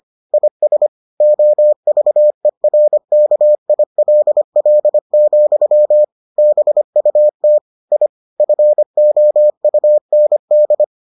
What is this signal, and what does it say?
Morse code, heard as a keyed tone, about 25 wpm, 600 Hz. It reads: IS OVERKILL, BUT I FOUND